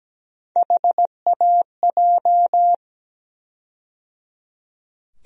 Morse code, heard as a keyed tone, about 17 wpm, 700 Hz. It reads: HAJ
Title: USA Call Sign Postfixes - 3 Letters 17wpm